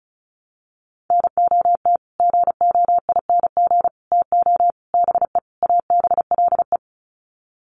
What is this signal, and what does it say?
Morse code, heard as a keyed tone, about 34 wpm, 700 Hz. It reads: NOT GOING TO BE ABLE
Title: Sets of 5 Words 35wpm